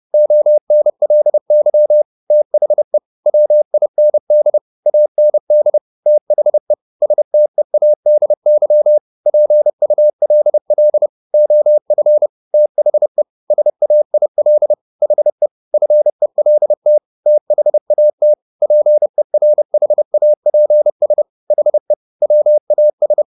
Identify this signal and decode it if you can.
Morse code, heard as a keyed tone, about 30 wpm, 600 Hz. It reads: ONLY THE WIND AND THE STEADY PULL OF THE SAIL HE FELT THAT PERHAPS HE WAS